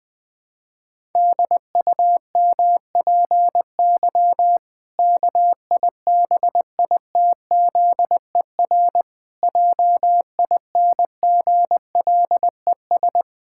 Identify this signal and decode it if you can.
Morse code, heard as a keyed tone, about 20 wpm, 700 Hz. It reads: DUMPY KIBITZER JINGLES
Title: Pangram Challenge v2 - 20wpm